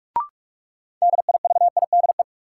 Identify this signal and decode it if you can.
Morse code, heard as a keyed tone, about 47 wpm, 700 Hz. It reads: DIVIDE
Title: Sentences from Top 1600 Words - New Word Drill - 45wpm